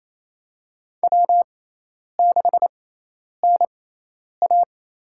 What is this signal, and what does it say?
Morse code, heard as a keyed tone, about 28 wpm, spaced as for 10 wpm, 700 Hz. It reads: W6NA
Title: USA Call Signs - 4 Characters - 6x Character Spacing - 28wpm